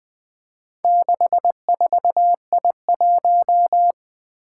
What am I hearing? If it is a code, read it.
Morse code, heard as a keyed tone, about 20 wpm, 700 Hz. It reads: 64I1
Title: Four Challenging Letter-Number Combinations 20wpm